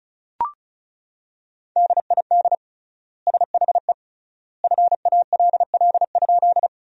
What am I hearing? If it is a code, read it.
Morse code, heard as a keyed tone, about 36 wpm, 700 Hz. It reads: DID SHE FALL?
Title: 3x Word-Spacing - Sentences from Top 500 Words - 35wpm